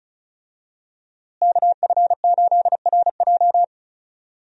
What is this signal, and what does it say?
Morse code, heard as a keyed tone, about 34 wpm, 700 Hz. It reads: KF8RJ
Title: USA Call Signs - 5 Characters - Repeated 3x - 35wpm